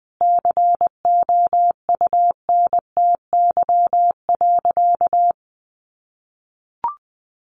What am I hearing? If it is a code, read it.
Morse code, heard as a keyed tone, about 20 wpm, 700 Hz. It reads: COUNTY.
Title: Sentences from Top 1500 Words 20wpm